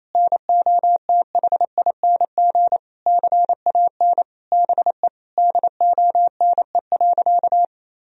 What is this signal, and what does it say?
Morse code, heard as a keyed tone, about 28 wpm, 700 Hz. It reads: NOTHING CAN BE DONE.
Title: Sentences from Top 1000 Words - Review Part 1 - 28wpm